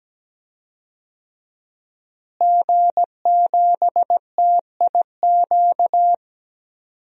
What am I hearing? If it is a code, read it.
Morse code, heard as a keyed tone, about 17 wpm, 700 Hz. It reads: G7TIQ
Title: DXCC-100 Call Signs - 5 Characters - Repeated 3x - 17wpm